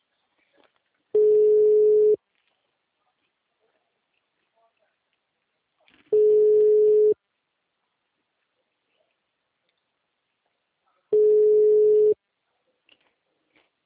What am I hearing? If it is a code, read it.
Ringback tone (Europe, 425 Hz).